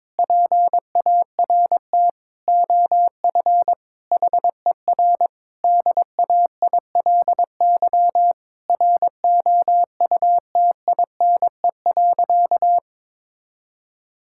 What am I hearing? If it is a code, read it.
Morse code, heard as a keyed tone, about 22 wpm, 700 Hz. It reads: PART OF HER DAILY ROUTINE.
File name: Difficult Sentences with Hints - 22wpm